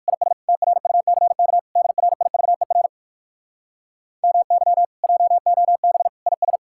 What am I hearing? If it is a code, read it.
Morse code, heard as a keyed tone, about 67 wpm, spaced as for 59 wpm, 700 Hz. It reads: IS TRUCK DRIVER MY JOB IS